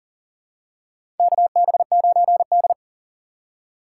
Morse code, heard as a keyed tone, about 40 wpm, 700 Hz. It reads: KB9D
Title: USA Call Signs - 4 Characters - 3x Character Spacing - 40wpm